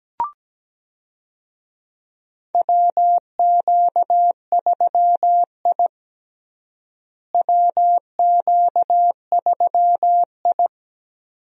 Morse code, heard as a keyed tone, about 17 wpm, 700 Hz. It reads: WQ3I WQ3I
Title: USA Call Signs - 4 Characters - Repeated 2x - 17wpm